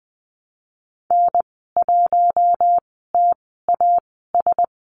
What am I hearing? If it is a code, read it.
Morse code, heard as a keyed tone, about 20 wpm, spaced as for 15 wpm, 700 Hz. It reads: N1TAS
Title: 048 Lesson 9g - Number 4 - Call Signs 20wpm